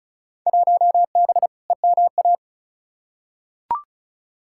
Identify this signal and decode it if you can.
Morse code, heard as a keyed tone, about 36 wpm, 700 Hz. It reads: 1B EMA
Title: Field Day Exchanges - 35wpm